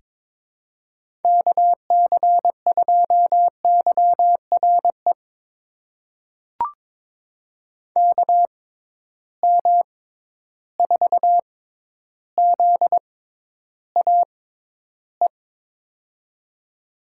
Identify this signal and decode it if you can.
Morse code, heard as a keyed tone, about 22 wpm, spaced as for 11 wpm, 700 Hz. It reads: KC2YRE KM4ZAE
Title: USA Call Signs - 6 Characters - 6x Character Spacing - 22wpm